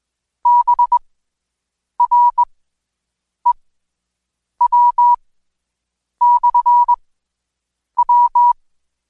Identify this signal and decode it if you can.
Morse code, heard as a keyed tone, about 21 wpm, spaced as for 7 wpm, 970 Hz. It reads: BREW/W
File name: Code Quick / AudioDownld2 / Seg113-Day18-DayDrill 18